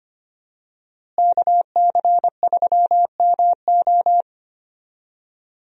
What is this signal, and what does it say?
Morse code, heard as a keyed tone, about 25 wpm, 700 Hz. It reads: KC3MO
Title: USA Call Signs - 5 Characters - Repeated 3x - 25wpm